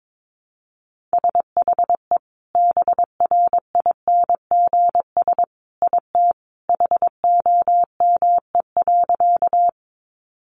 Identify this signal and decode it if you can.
Morse code, heard as a keyed tone, about 22 wpm, 700 Hz. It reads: SHE BRINGS IT HOME.
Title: Easy Phrases 22wpm